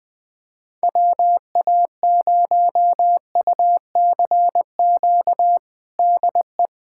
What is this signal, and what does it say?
Morse code, heard as a keyed tone, about 21 wpm, 700 Hz. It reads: WA0UCQ DE
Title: USA Callsign DE Callsign - 6 Characters - 20wpm